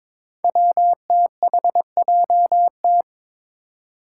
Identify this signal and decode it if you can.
Morse code, heard as a keyed tone, about 22 wpm, 700 Hz. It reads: WTHJT
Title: Five Challenging Letter-Number Combinations 22wpm